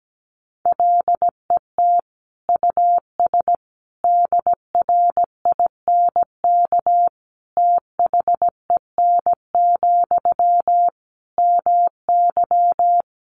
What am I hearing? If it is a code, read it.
Morse code, heard as a keyed tone, about 17 wpm, 700 Hz. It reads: LET US DRINK THEN, MY
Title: Sentences from Top 1500 Words 17wpm